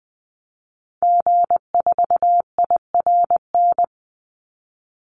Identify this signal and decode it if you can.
Morse code, heard as a keyed tone, about 19 wpm, 700 Hz. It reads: G4IRN
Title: Call Signs Repeated 3x - 20wpm